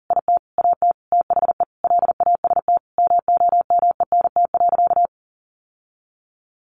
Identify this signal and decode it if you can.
Morse code, heard as a keyed tone, about 40 wpm, 700 Hz. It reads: IT AT THE LAST MOMENT.
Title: Sentences from Top 1400 Words 40wpm